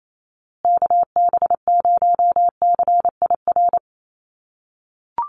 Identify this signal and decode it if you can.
Morse code, heard as a keyed tone, about 28 wpm, 700 Hz. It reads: KB0CIR
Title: USA Call Signs Repeated 3x 28wpm